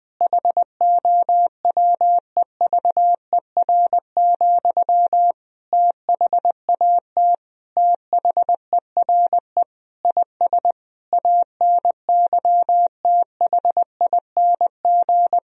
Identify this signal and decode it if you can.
Morse code, heard as a keyed tone, about 21 wpm, 700 Hz. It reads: HOWEVER, THAT THERE IS ANYTHING
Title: Sentences from Top 1000 Words - Review Part 3 - 20wpm